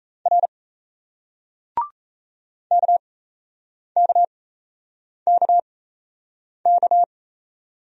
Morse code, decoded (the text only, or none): R KKKK